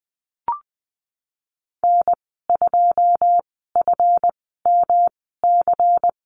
Morse code, heard as a keyed tone, about 20 wpm, spaced as for 15 wpm, 700 Hz. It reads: N2FMC
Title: 173 Lesson 28g - Number 9 - Call Signs 20wpm